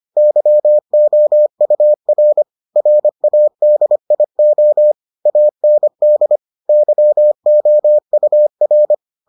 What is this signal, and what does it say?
Morse code, heard as a keyed tone, about 25 wpm, 600 Hz. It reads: YOUR RADIO AND YOUR